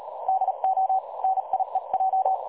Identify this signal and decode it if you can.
Morse code, heard as a keyed tone, about 38 wpm, spaced as for 33 wpm, 760 Hz. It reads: DO NIEJ
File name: dlazaawansowanych / nagraniazpasma / websdr recording 3565.0kHz